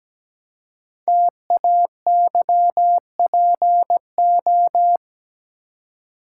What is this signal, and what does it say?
Morse code, heard as a keyed tone, about 17 wpm, 700 Hz. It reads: TAYPO
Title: Code Groups - 5 Letters - 17wpm